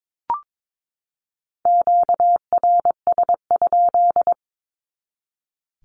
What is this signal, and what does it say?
Morse code, heard as a keyed tone, about 22 wpm, 700 Hz. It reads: QRS?
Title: Q-Codes 22wpm